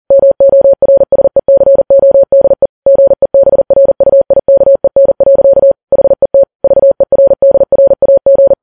Morse code, heard as a keyed tone, about 40 wpm, 550 Hz. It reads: MORSECODE GEBRUIKEN. HET VERDRAG
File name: Files Web tls / Morse on9fv / morse40kps